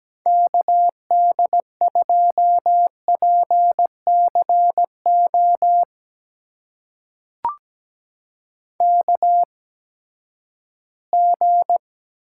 Morse code, heard as a keyed tone, about 17 wpm, spaced as for 12 wpm, 700 Hz. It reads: KD2PCO KG
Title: USA Call Signs - 8x Character Spacing - 17wpm